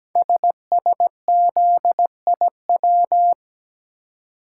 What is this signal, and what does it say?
Morse code, heard as a keyed tone, about 17 wpm, 700 Hz. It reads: SSZIW
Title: Code Groups - 5 Letters - 17wpm